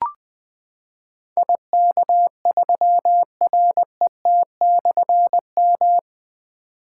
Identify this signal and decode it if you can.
Morse code, heard as a keyed tone, about 20 wpm, 700 Hz. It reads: IK3RET/M
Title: Difficult Call Signs 20wpm